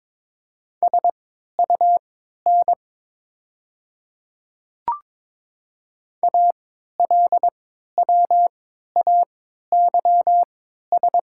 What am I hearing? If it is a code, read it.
Morse code, heard as a keyed tone, about 22 wpm, spaced as for 12 wpm, 690 Hz. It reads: SUN ALWAYS
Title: 3x Character Spacing - Top 300 Words - 22wpm